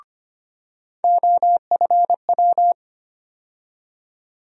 Morse code, heard as a keyed tone, about 25 wpm, 700 Hz. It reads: OFW
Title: USA Call Sign Postfixes - 3 Letters 25wpm